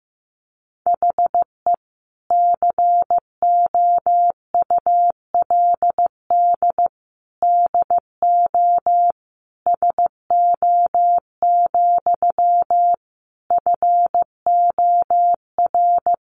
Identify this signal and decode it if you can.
Morse code, heard as a keyed tone, about 15 wpm, 700 Hz. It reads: HE COULD DO SO, FOR
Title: Sentences from Top 1500 Words 15wpm